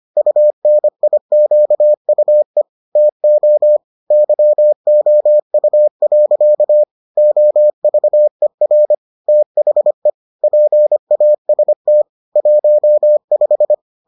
Morse code, heard as a keyed tone, about 25 wpm, 600 Hz. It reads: UNIQUE TO YOU. OVER THE PAST 15